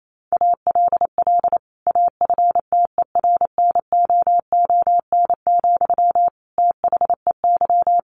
Morse code, heard as a keyed tone, about 28 wpm, 700 Hz. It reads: ALL AFTERNOON, THEY
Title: Sentences from Top 1500 Words 28wpm